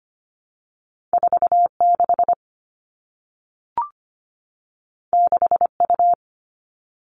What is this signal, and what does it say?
Morse code, heard as a keyed tone, about 25 wpm, 700 Hz. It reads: 4 6 6 U
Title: Two Challenging Letter-Number Combinations 25wpm